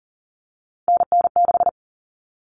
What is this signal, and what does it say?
Morse code, heard as a keyed tone, about 40 wpm, 700 Hz. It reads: NN6
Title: Common Cut Numbers - 3-Digit 40wpm